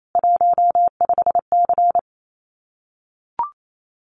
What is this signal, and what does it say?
Morse code, heard as a keyed tone, about 28 wpm, 700 Hz. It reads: 15C
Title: Common QSO Phrases 28wpm